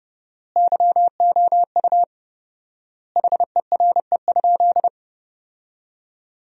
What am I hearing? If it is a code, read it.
Morse code, heard as a keyed tone, about 29 wpm, 700 Hz. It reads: YOU HERE?
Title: 4x Word-Spacing - Sentences from Top 200 Words - 30wpm